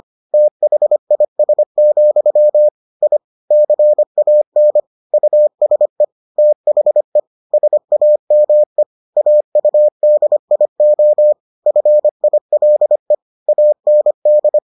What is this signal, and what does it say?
Morse code, heard as a keyed tone, about 25 wpm, 600 Hz. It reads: THIS, I CAN USE THE SAME AUDIO FILE AND